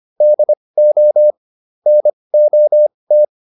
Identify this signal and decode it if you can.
Morse code, heard as a keyed tone, about 25 wpm, spaced as for 20 wpm, 600 Hz. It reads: DO NOT